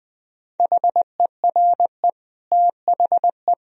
Morse code, heard as a keyed tone, about 19 wpm, 700 Hz. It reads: HERE THE
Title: Sentences from Top 1000 Words - Review Part 2 - 20wpm